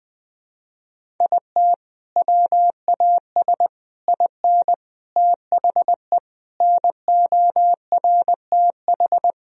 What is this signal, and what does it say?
Morse code, heard as a keyed tone, about 20 wpm, 700 Hz. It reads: IT WAS IN THE NORTH
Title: Sentences from Top 1000 Words - Review Part 1 - 20wpm